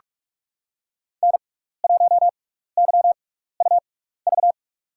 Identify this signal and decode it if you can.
Morse code, heard as a keyed tone, about 44 wpm, spaced as for 15 wpm, 700 Hz. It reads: N1YUV
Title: USA Call Signs - 6x Character Spacing - 45wpm